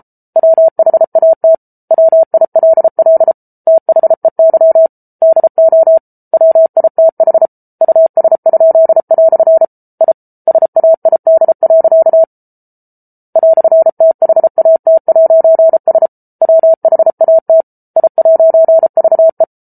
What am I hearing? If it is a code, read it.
Morse code, heard as a keyed tone, about 34 wpm, 650 Hz. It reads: WHAT WILL THEY DO WITH US?" I SAID. "THAT'S WHAT I'VE